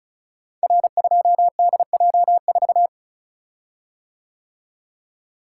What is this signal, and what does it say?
Morse code, heard as a keyed tone, about 36 wpm, 700 Hz. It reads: R2DJ4
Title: Code Groups - 5 Letter:Numbers - 35wpm